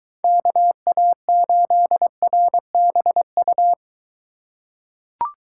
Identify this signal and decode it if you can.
Morse code, heard as a keyed tone, about 23 wpm, 700 Hz. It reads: KA8RBU